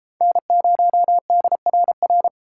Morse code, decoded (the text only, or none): N0DRR